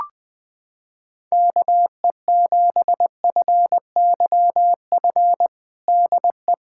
Morse code, heard as a keyed tone, about 21 wpm, 700 Hz. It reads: KE7FYF DE
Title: USA Callsign DE Callsign - 6 Characters - 20wpm